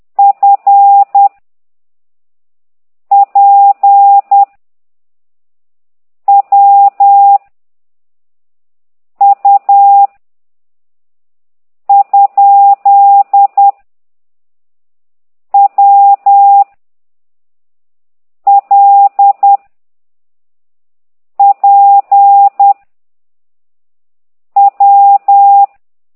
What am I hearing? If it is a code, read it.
Morse code, heard as a keyed tone, about 10 wpm, 800 Hz. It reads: F P W U ? W L P W
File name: ct2idw / exerc11